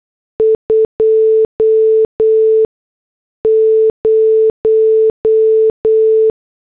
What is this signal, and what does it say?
Morse code, heard as a keyed tone, about 8 wpm, spaced as for 6 wpm, 440 Hz. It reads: 20
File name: morse / headlines